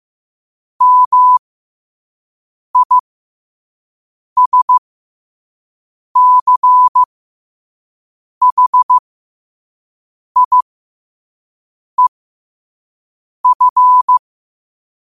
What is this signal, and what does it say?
Morse code, heard as a keyed tone, about 15 wpm, spaced as for 5 wpm, 1000 Hz. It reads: MISCHIEF
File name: qotdfiles / 5wpm